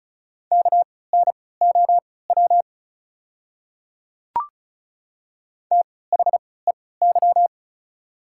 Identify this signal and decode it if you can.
Morse code, heard as a keyed tone, about 34 wpm, spaced as for 20 wpm, 690 Hz. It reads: KNOW THEY
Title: 3x Character Spacing - Top 100 Words - 35wpm